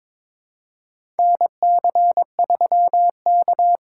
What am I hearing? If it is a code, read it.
Morse code, heard as a keyed tone, about 22 wpm, 700 Hz. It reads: NC3K
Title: USA Call Signs - 4 Characters - 3x Character Spacing - 22wpm